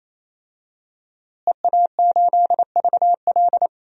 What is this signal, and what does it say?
Morse code, heard as a keyed tone, about 28 wpm, 700 Hz. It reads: EA8VL
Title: DXCC-100 Call Signs - 5 Characters - Repeated 3x - 28wpm